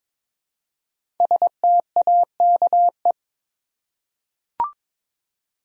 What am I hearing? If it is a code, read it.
Morse code, heard as a keyed tone, about 22 wpm, 700 Hz. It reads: STAKE
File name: 5 Letter Words - First 3 Letters Given - 22wpm